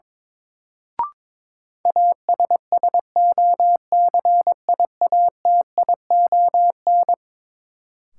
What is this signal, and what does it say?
Morse code, heard as a keyed tone, about 22 wpm, 700 Hz. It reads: ASSOCIATION
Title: Sentences from Top 1100 Words - New Word Drill Alt - 22wpm